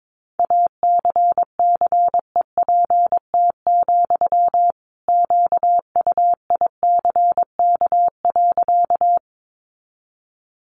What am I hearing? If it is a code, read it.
Morse code, heard as a keyed tone, about 22 wpm, 700 Hz. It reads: ACCEPT, QUICK.
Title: Sentences from Top 1300 Words 22wpm